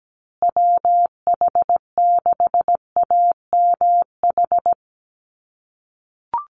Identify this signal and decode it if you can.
Morse code, heard as a keyed tone, about 17 wpm, 700 Hz. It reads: WH6AMH
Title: USA Call Signs Repeated 3x 17wpm